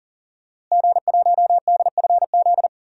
Morse code, heard as a keyed tone, about 40 wpm, 700 Hz. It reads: G1DFZ
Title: DXCC-100 Call Signs - 5 Characters - Repeated 3x - 40wpm